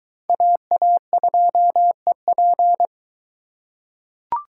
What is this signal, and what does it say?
Morse code, heard as a keyed tone, about 22 wpm, 700 Hz. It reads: AA2EP